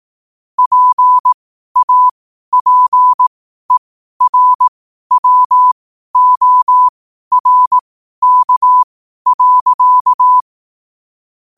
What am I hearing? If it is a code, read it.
Morse code, heard as a keyed tone, about 18 wpm, spaced as for 12 wpm, 1000 Hz. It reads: PAPERWORK.
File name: qotdfiles / 12wpm